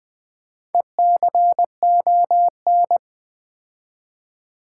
Morse code, heard as a keyed tone, about 20 wpm, 700 Hz. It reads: ECON
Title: Build A Word - 2 Letters At A Time - 8-Letter Words - 20wpm